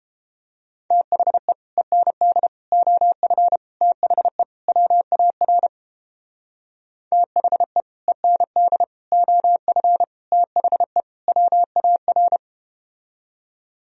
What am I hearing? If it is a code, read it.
Morse code, heard as a keyed tone, about 31 wpm, 700 Hz. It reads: THE END OF THE WAR THE END OF THE WAR